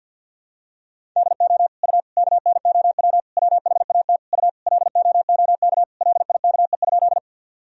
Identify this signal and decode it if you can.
Morse code, heard as a keyed tone, about 51 wpm, 700 Hz. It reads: DO U KNOW WHAT U LOOK LIKE?
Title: Sentences from Top 100 Words and Abbreviations 50wpm